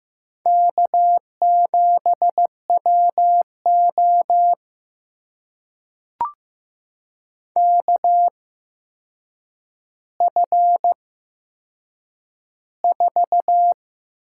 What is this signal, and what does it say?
Morse code, heard as a keyed tone, about 15 wpm, spaced as for 7 wpm, 700 Hz. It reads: K7WO KF4
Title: USA Call Signs - 8x Character Spacing - 15wpm